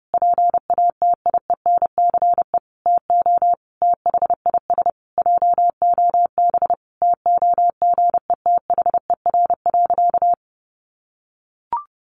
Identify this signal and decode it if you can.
Morse code, heard as a keyed tone, about 28 wpm, 700 Hz. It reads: PATIENCE TO THIS JOB TOGETHER.
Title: Sentences from Top 1300 Words 30wpm